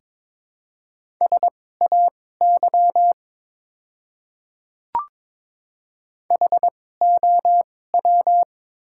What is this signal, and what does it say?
Morse code, heard as a keyed tone, about 22 wpm, spaced as for 16 wpm, 700 Hz. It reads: SAY HOW
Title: 2x Character Spacing - Top 200 Words - 22wpm